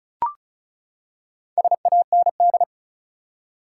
Morse code, heard as a keyed tone, about 36 wpm, 700 Hz. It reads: SAND
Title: Four Letter Words - 35wpm